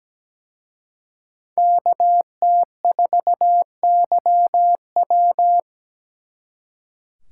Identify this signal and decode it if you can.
Morse code, heard as a keyed tone, about 20 wpm, 700 Hz. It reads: KT4YW
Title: Speed Racing - USA 2x2 Call Signs 17wpm